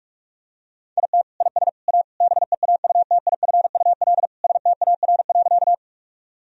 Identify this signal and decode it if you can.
Morse code, heard as a keyed tone, about 42 wpm, 700 Hz. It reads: IT IS A BEAUTIFUL STAR.
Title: Sentences from Top 1000 Words - Review Part 2 - 45wpm